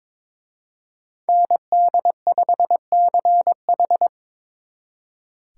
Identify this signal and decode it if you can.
Morse code, heard as a keyed tone, about 26 wpm, 700 Hz. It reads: ND5CH
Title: Speed Racing - USA 2x2 Call Signs 22wpm